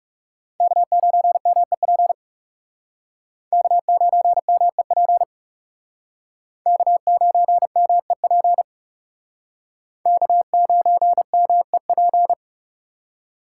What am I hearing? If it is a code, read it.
Morse code, heard as a keyed tone, about 37 wpm, 700 Hz. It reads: K9MEP K9MEP K9MEP K9MEP
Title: Speed Racing - USA 1x3 Call Signs 30wpm